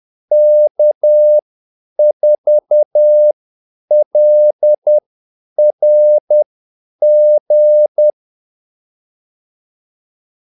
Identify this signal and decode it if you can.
Morse code, heard as a keyed tone, about 10 wpm, spaced as for 8 wpm, 600 Hz. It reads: K4LRG
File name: Callsigns-10-8wpm